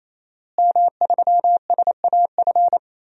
Morse code, heard as a keyed tone, about 28 wpm, 700 Hz. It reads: M3SAF
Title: DXCC-100 Call Signs - 5 Characters - Repeated 3x - 28wpm